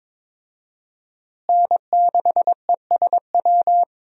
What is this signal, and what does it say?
Morse code, heard as a keyed tone, about 22 wpm, 700 Hz. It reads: N6ESW
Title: USA Call Signs - 5 Characters - Repeated 2x - 22wpm